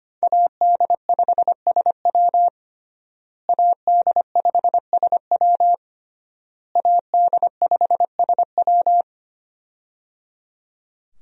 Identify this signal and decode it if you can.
Morse code, heard as a keyed tone, about 25 wpm, 700 Hz. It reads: AD5SW AD5SW AD5SW
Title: USA Call Signs - 5 Characters - Repeated 3x - 25wpm